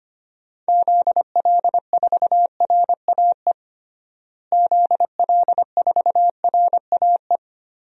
Morse code, heard as a keyed tone, about 25 wpm, 700 Hz. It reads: ZL4RAE ZL4RAE
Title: DXCC-100 Call Signs - 6 Characters - Repeated 3x - 25wpm